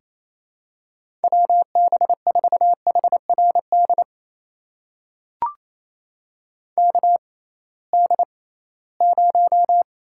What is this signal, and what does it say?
Morse code, heard as a keyed tone, about 28 wpm, spaced as for 18 wpm, 700 Hz. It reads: WB4HRD KD0
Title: USA Call Signs - 6 Characters - 6x Character Spacing - 28wpm